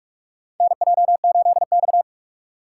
Morse code, heard as a keyed tone, about 44 wpm, 700 Hz. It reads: NJ8X
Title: USA Call Signs - 4 Characters - 3x Character Spacing - 45wpm